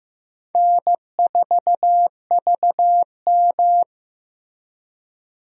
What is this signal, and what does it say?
Morse code, heard as a keyed tone, about 15 wpm, 700 Hz. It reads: N4VM
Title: USA Call Signs - 4 Characters - Repeated 3x - 15wpm